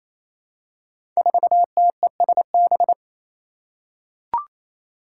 Morse code, heard as a keyed tone, about 29 wpm, 700 Hz. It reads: 4TESB
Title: Five Challenging Letter-Number Combinations 28wpm